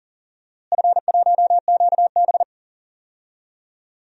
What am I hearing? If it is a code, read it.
Morse code, heard as a keyed tone, about 39 wpm, 700 Hz. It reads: F1QB
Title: DXCC-100 Call Signs - 4 Characters - Repeated 2x - 40wpm